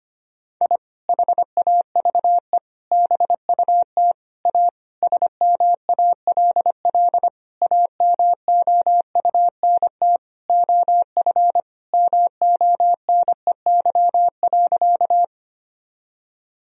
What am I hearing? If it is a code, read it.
Morse code, heard as a keyed tone, about 25 wpm, 700 Hz. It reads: I HAVE BUT A SMALL AMOUNT OF MONEY.
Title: Sentences from Top 1000 Words - Review Part 4 - 25wpm